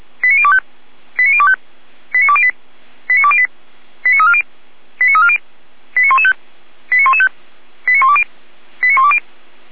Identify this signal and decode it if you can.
Five-tone selective call (ZVEI), heard as quick runs of five tones, tones of about 70 ms. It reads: 89025 89025 89208 89208 89340 89340 89115 89115 89120 89120